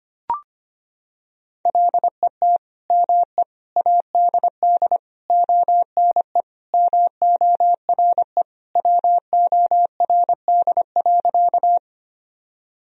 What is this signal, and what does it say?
Morse code, heard as a keyed tone, about 25 wpm, 700 Hz. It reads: LET ME ADD ONE MORE WORD.
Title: Sentences from Top 1000 Words - Review Part 1 - 25wpm